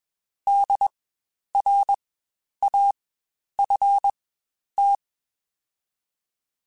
Morse code, heard as a keyed tone, about 20 wpm, spaced as for 10 wpm, 800 Hz. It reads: DRAFT
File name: Spaced Alphas / Spaced Alphas mp3s / Ds sp @20